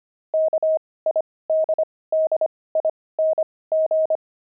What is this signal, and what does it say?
Morse code, heard as a keyed tone, about 25 wpm, spaced as for 18 wpm, 630 Hz. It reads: KIDDING
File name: pangram-fs-18-cs-25-kidding